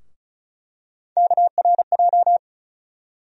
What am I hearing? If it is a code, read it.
Morse code, heard as a keyed tone, about 36 wpm, 700 Hz. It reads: KRJ
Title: USA Call Sign Postfixes - 3 Letters 35wpm